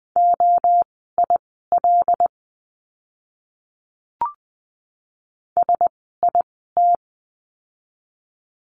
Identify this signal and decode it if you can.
Morse code, heard as a keyed tone, about 20 wpm, spaced as for 15 wpm, 700 Hz. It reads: OIL SIT
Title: 101 Lesson 17f - Letter u - Words 20wpm